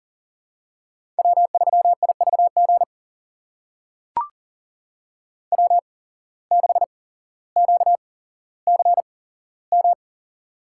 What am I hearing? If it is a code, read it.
Morse code, heard as a keyed tone, about 40 wpm, spaced as for 17 wpm, 700 Hz. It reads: W3IVG W6QCM
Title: USA Call Signs - 8x Character Spacing - 40wpm